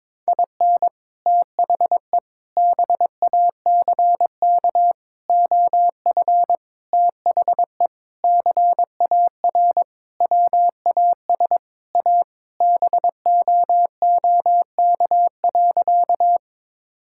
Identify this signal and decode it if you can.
Morse code, heard as a keyed tone, about 22 wpm, 700 Hz. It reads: IN THE BACK OF THE CAR WAS A BOOK.
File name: practice / 22wpm / Sentences with 4-Word Phrases - 22wpm